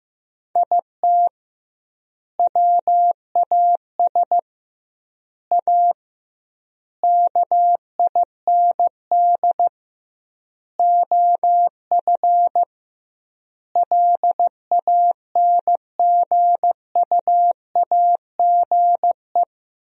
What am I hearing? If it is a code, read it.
Morse code, heard as a keyed tone, about 15 wpm, 700 Hz. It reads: IT WAS A KIND OF LANGUAGE
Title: 2x Word-Spacing - Sentences from Top 500 Words - 15wpm